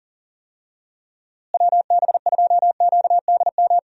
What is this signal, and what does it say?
Morse code, heard as a keyed tone, about 37 wpm, 700 Hz. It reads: WB2QDM
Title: USA Call Signs - 6 Characters - Repeated 3x - 40wpm